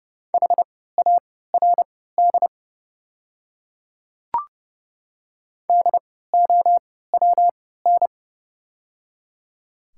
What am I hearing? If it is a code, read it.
Morse code, heard as a keyed tone, about 30 wpm, spaced as for 17 wpm, 700 Hz. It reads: HARD DOWN
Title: 3x Character Spacing - Top 300 Words - 30wpm